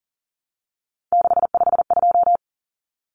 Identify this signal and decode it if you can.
Morse code, heard as a keyed tone, about 40 wpm, 700 Hz. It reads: 652
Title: Numbers - 3-Digit 40wpm